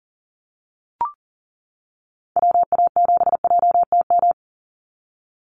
Morse code, heard as a keyed tone, about 41 wpm, 700 Hz. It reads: WA7JTM
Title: NA SOTA Chaser Callsigns 40wpm